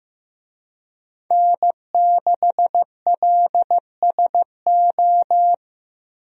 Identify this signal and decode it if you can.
Morse code, heard as a keyed tone, about 16 wpm, 700 Hz. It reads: N6LSO
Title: USA Call Signs - 5 Characters - Repeated 2x - 15wpm